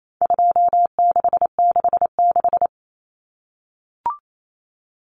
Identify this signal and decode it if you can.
Morse code, heard as a keyed tone, about 28 wpm, 700 Hz. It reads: 2666
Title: Common Cut Numbers - 4-Digit 28wpm